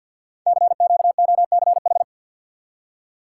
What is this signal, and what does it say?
Morse code, heard as a keyed tone, about 52 wpm, 700 Hz. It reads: CQOCH
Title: Code Groups - 5 Letter:Numbers - 50wpm